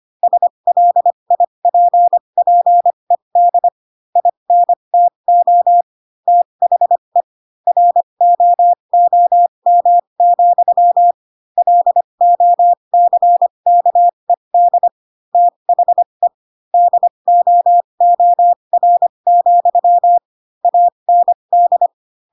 Morse code, heard as a keyed tone, about 25 wpm, spaced as for 22 wpm, 700 Hz. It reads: SLIPPED INTO THE ROOM, LOCKED THE DOOR, AND
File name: Pinocchio / Pinocchio ch4 18 25 ik1hge